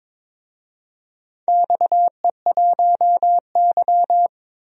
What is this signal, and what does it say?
Morse code, heard as a keyed tone, about 23 wpm, 690 Hz. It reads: XE1Y
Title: DXCC-100 Call Signs - 4 Characters - 22wpm